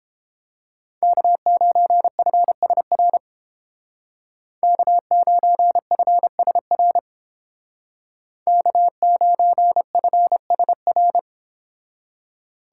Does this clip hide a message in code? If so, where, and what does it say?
Yes: Morse code from 1.0 to 11.2 s, heard as a keyed tone, about 29 wpm, 700 Hz. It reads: K9FSR K9FSR K9FSR